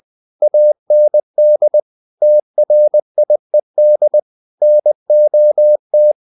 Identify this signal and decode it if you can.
Morse code, heard as a keyed tone, about 20 wpm, 600 Hz. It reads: AND TRIED NOT
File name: the old man and the sea / the old man and the sea part 1 20wpm